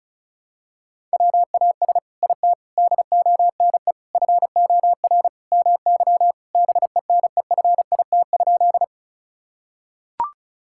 Morse code, heard as a keyed tone, about 34 wpm, 700 Hz. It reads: WAS IT DONE FOR MY BENEFIT?
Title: Sentences from Top 1000 Words - Review Part 6 - 35wpm